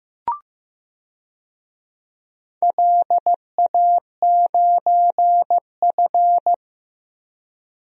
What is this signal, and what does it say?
Morse code, heard as a keyed tone, about 16 wpm, 700 Hz. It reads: LA9F
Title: DXCC-100 Call Signs - 4 Characters - Repeated 2x - 15wpm